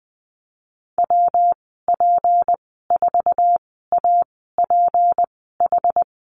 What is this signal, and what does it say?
Morse code, heard as a keyed tone, about 20 wpm, spaced as for 14 wpm, 700 Hz. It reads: WP4APH
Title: 161 Lesson 26g - Letter g - Call Signs 20wpm